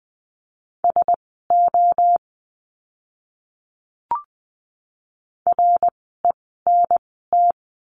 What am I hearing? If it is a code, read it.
Morse code, heard as a keyed tone, about 20 wpm, spaced as for 14 wpm, 700 Hz. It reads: SO RENT
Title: 054 Lesson 10f - Letter r - Words 20wpm